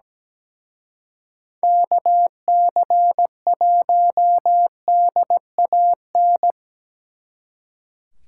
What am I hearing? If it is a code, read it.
Morse code, heard as a keyed tone, about 20 wpm, 700 Hz. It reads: KC1DAN
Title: Speed Racing - USA Call Signs 17wpm